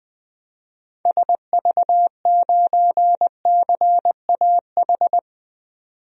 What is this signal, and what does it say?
Morse code, heard as a keyed tone, about 21 wpm, 700 Hz. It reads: SV9CAH
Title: DXCC-100 Call Signs - 6 Characters - 20wpm